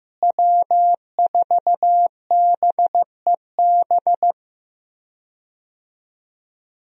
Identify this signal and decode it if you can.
Morse code, heard as a keyed tone, about 16 wpm, 700 Hz. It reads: W4BEB
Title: USA Call Signs - 5 Characters - Repeated 2x - 15wpm